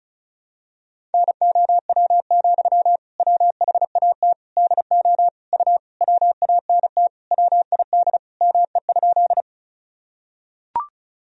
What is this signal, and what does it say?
Morse code, heard as a keyed tone, about 34 wpm, 700 Hz. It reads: NOW, WHAT DO U WANT WID ME?
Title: Sentences from Top 200 Words and Abbreviations 35wpm